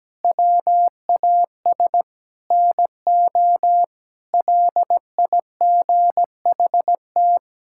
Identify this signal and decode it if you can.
Morse code, heard as a keyed tone, about 17 wpm, 700 Hz. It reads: WAS NO LIGHT
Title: Sentences from Top 1000 Words - Review Part 5 - 17wpm